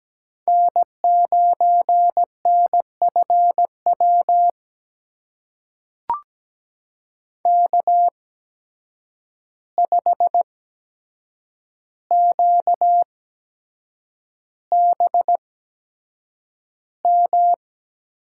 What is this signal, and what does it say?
Morse code, heard as a keyed tone, about 17 wpm, spaced as for 7 wpm, 700 Hz. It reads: N9NFW K5QBM
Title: USA Call Signs - 5 Characters - 8x Character Spacing - 17wpm